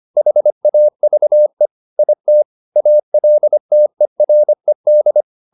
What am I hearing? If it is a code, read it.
Morse code, heard as a keyed tone, about 25 wpm, 600 Hz. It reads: HAVE IT ALTERED